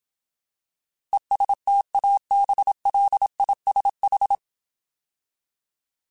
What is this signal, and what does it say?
Morse code, heard as a keyed tone, about 27 wpm, 800 Hz. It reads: ESTABLISH
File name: Alphabetical / Alphabetical mp3s / Es @25